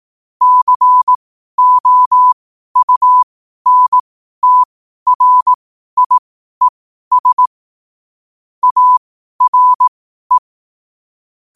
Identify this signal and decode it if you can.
Morse code, heard as a keyed tone, about 18 wpm, spaced as for 12 wpm, 1000 Hz. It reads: COUNTRIES ARE
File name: qotdfiles / 12wpm